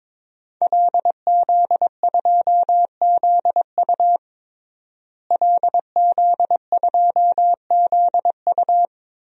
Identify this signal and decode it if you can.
Morse code, heard as a keyed tone, about 22 wpm, 700 Hz. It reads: LZ2ZU LZ2ZU
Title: DXCC-100 Call Signs - 5 Characters - Repeated 2x - 22wpm